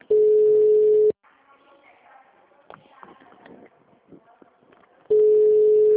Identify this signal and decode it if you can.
Ringback tone (Europe, 425 Hz), with speech.